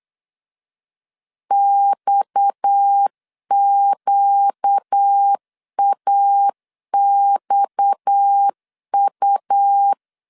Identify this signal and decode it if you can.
Morse code, heard as a keyed tone, about 8 wpm, 800 Hz. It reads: XQAXU
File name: LEZIONI / 6 PKX0 / 6 PKX 8wpm 3041